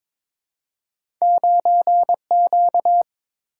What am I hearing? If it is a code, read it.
Morse code, heard as a keyed tone, about 23 wpm, 700 Hz. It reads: 9Q
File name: practice / 22wpm / Two Letter-Number Combinations - 22wpm